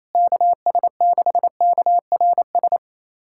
Morse code, heard as a keyed tone, about 27 wpm, 700 Hz. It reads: KS6KRS
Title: USA Callsign DE Callsign 28wpm